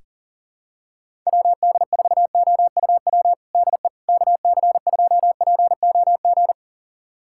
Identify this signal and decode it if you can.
Morse code, heard as a keyed tone, about 39 wpm, 700 Hz. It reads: WD4OUW DE KC2POG
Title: USA Callsign DE Callsign 40wpm